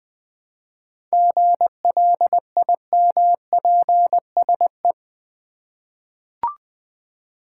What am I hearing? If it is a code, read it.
Morse code, heard as a keyed tone, about 20 wpm, 700 Hz. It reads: GLIMPSE
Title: 7 Letter Words - First 3 Letters Given - 20wpm